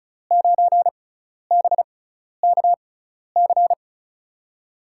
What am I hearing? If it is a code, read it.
Morse code, heard as a keyed tone, about 34 wpm, spaced as for 12 wpm, 700 Hz. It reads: 9BKC
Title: USA Call Signs - 6x Character Spacing - 35wpm